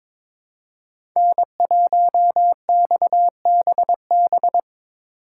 Morse code, heard as a keyed tone, about 23 wpm, 700 Hz. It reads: N1XBB
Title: USA Call Signs - 5 Characters - 8x Character Spacing - 22wpm